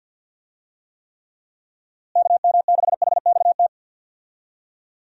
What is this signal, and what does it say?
Morse code, heard as a keyed tone, about 51 wpm, 700 Hz. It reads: KM6HXT
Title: USA Call Signs - 6 Characters - Repeated 2x - 50wpm